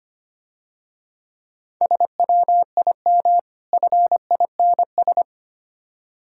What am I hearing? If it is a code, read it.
Morse code, heard as a keyed tone, about 25 wpm, 690 Hz. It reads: SWIM FINS
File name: practice / 25wpm / Difficult Words - Speed-Racing with Hints - 25wpm